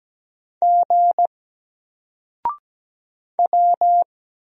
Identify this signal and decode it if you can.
Morse code, heard as a keyed tone, about 17 wpm, 700 Hz. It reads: G W
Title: Single Letter-Number 17wpm